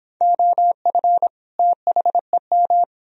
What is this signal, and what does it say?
Morse code, heard as a keyed tone, about 30 wpm, 700 Hz. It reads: OF THEM